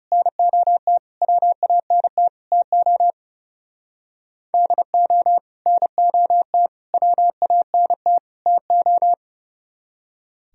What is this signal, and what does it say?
Morse code, heard as a keyed tone, about 37 wpm, 700 Hz. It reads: NOT WANT TO DO NOT WANT TO
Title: Speed-Racing - Sets of 4 Words 30wpm